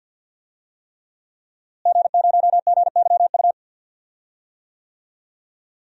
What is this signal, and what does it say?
Morse code, heard as a keyed tone, about 52 wpm, 700 Hz. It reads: G0GYU
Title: DXCC-100 Call Signs - 5 Characters - Repeated 3x - 50wpm